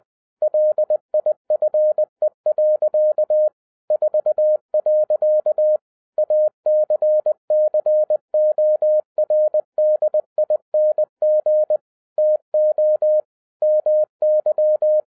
Morse code, heard as a keyed tone, about 20 wpm, 610 Hz. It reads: LIFE. 4. ACCORDING TO MY
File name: 2011-10-mp3 / n13-TIPS FROM MY 34 YEARS OF EXPERIENCE 20wpm